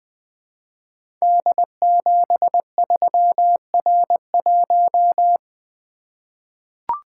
Morse code, heard as a keyed tone, about 21 wpm, 700 Hz. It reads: D73R1
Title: Code Groups - 5 Letter:Numbers - 20wpm